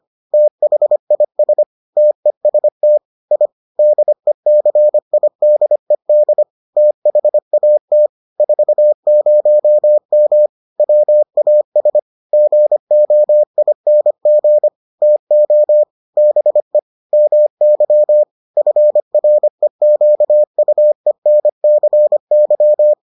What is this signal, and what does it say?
Morse code, heard as a keyed tone, about 25 wpm, 600 Hz. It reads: THIS TEST I DECIDED THAT 40M WAS GOING TO BE MY FREQUENCY